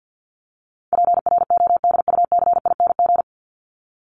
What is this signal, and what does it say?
Morse code, heard as a keyed tone, about 48 wpm, 700 Hz. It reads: PRODUCING
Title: ING Suffix - Top 100 Words with Base Words 50wpm